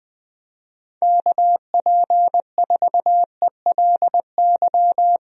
Morse code, heard as a keyed tone, about 20 wpm, 700 Hz. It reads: KP4ELY
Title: USA Call Signs - 6 Characters - Repeated 3x - 20wpm